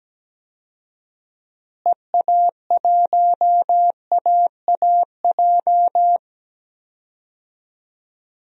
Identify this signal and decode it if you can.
Morse code, heard as a keyed tone, about 17 wpm, 700 Hz. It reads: EA1AAJ
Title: DXCC-100 Call Signs - 6 Characters - Repeated 2x - 17wpm